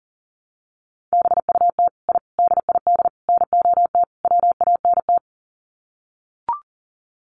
Morse code, heard as a keyed tone, about 40 wpm, 700 Hz. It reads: BUT I DID NOT WANT
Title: Sets of 5 Words Encore 40wpm